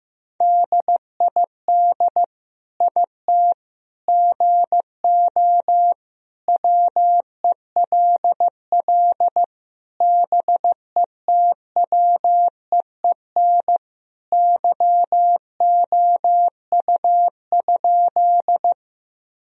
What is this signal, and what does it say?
Morse code, heard as a keyed tone, about 15 wpm, 700 Hz. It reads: DID IT GO WELL BETWEEN YOU?
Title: Sentences from Top 1000 Words - Review Part 1 - 15wpm